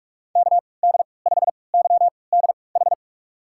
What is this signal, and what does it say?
Morse code, heard as a keyed tone, about 47 wpm, spaced as for 26 wpm, 700 Hz. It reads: KD5YDH
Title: USA Call Signs - 6 Characters - 3x Character Spacing - 45wpm